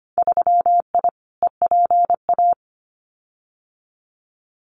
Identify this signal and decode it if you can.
Morse code, heard as a keyed tone, about 25 wpm, 700 Hz. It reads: 3I EPA
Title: Winter Field Day Exchanges 25wpm